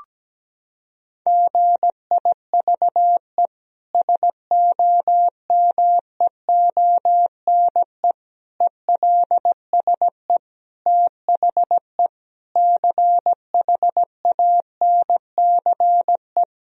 Morse code, heard as a keyed tone, about 17 wpm, 700 Hz. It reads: GIVE SOMEONE ELSE THE CHANCE
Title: Sentences from Top 1000 Words - Review Part 4 - 17wpm